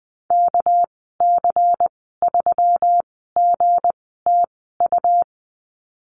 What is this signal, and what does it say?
Morse code, heard as a keyed tone, about 20 wpm, spaced as for 15 wpm, 700 Hz. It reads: KC3GTU
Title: 200 Lesson 32g - Letter k - Call Signs 20wpm